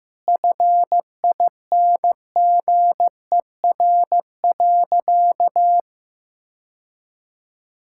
Morse code, heard as a keyed tone, about 15 wpm, 700 Hz. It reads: FINGER.
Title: Sentences from Top 1600 Words 15wpm